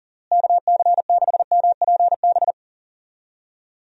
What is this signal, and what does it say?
Morse code, heard as a keyed tone, about 40 wpm, 700 Hz. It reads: KC6MPB
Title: USA Call Signs - 6 Characters - Repeated 3x - 40wpm